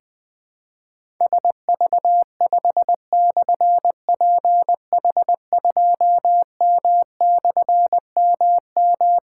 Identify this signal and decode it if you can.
Morse code, heard as a keyed tone, about 20 wpm, 700 Hz. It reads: SV5/PH2M/MM
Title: Difficult Call Signs 20wpm